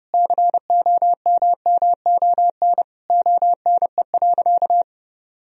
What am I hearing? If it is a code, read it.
Morse code, heard as a keyed tone, about 30 wpm, 700 Hz. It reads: COMMON ONE.
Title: Sentences from Top 1000 Words - Review Part 4 - 30wpm